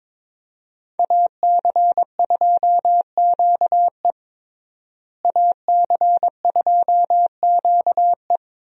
Morse code, heard as a keyed tone, about 22 wpm, 700 Hz. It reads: AC2QE AC2QE
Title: USA Call Signs - 5 Characters - Repeated 2x - 22wpm